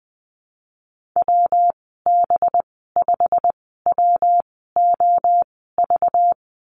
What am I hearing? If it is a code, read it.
Morse code, heard as a keyed tone, about 20 wpm, spaced as for 14 wpm, 700 Hz. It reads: WB5WOV
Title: 207 Lesson 33g - Letter j - Call Signs 20wpm